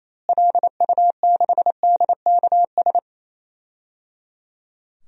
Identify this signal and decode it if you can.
Morse code, heard as a keyed tone, about 28 wpm, 700 Hz. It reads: LU6DKS
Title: DXCC-100 Call Signs - 6 Characters - Repeated 3x - 28wpm